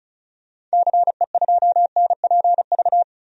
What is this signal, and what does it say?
Morse code, heard as a keyed tone, about 36 wpm, 700 Hz. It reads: CE2NPV
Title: DXCC-100 Call Signs - 6 Characters - 35wpm